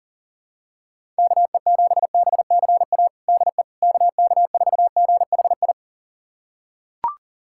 Morse code, heard as a keyed tone, about 40 wpm, 700 Hz. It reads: KE7BCA DE KK4GHI
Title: USA Callsign DE Callsign 40wpm